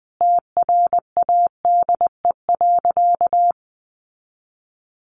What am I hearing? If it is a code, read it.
Morse code, heard as a keyed tone, about 19 wpm, 700 Hz. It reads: TRADE.
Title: Sentences from Top 1300 Words 20wpm